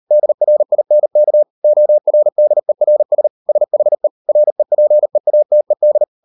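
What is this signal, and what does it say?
Morse code, heard as a keyed tone, about 38 wpm, 600 Hz. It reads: DRINK ORDERS SHE REPEATED
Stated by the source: Morse code practice at 38 WPM, sending Ken Follett's novel Never in English, from CW med Gnister.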